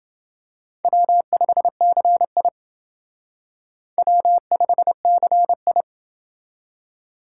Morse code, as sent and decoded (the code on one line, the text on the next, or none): .-- ..... -.-. .. / .-- ..... -.-. ..
W5CI W5CI